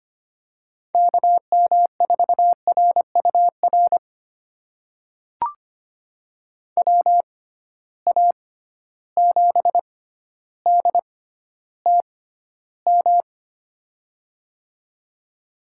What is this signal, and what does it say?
Morse code, heard as a keyed tone, about 25 wpm, spaced as for 13 wpm, 700 Hz. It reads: KM4RUR WA7DTM
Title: USA Call Signs - 6 Characters - 6x Character Spacing - 25wpm